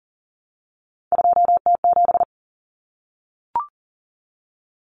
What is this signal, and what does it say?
Morse code, heard as a keyed tone, about 40 wpm, 700 Hz. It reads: 2T7
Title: Common Cut Numbers - 3-Digit 40wpm